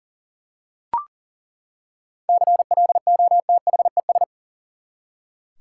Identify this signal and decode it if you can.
Morse code, heard as a keyed tone, about 40 wpm, 700 Hz. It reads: CLOTHES
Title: Seven Letter Words - 40wpm